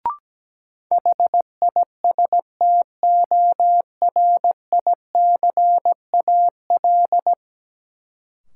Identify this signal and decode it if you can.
Morse code, heard as a keyed tone, about 17 wpm, 700 Hz. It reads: HISTORICAL
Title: Sentences from Top 600 Words - New Word Drill - 17wpm